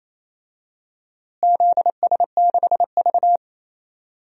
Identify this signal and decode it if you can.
Morse code, heard as a keyed tone, about 28 wpm, 700 Hz. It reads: ZS6V
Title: DXCC-100 Call Signs - 4 Characters - Repeated 3x - 28wpm